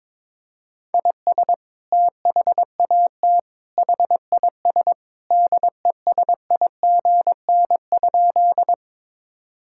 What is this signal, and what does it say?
Morse code, heard as a keyed tone, about 22 wpm, 700 Hz. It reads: IS THAT HIS DESIGN?
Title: Sentences from Top 1000 Words - Review Part 5 - 22wpm